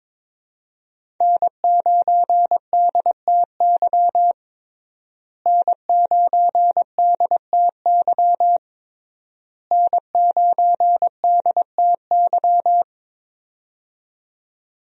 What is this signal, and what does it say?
Morse code, heard as a keyed tone, about 22 wpm, 700 Hz. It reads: N9DTY N9DTY N9DTY
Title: USA Call Signs - 5 Characters - Repeated 3x - 22wpm